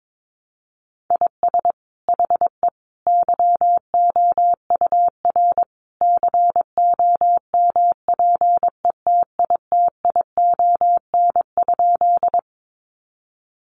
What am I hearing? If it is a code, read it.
Morse code, heard as a keyed tone, about 22 wpm, 700 Hz. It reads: IS HE YOUR COMPETITION?
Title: Sentences from Top 1000 Words 22wpm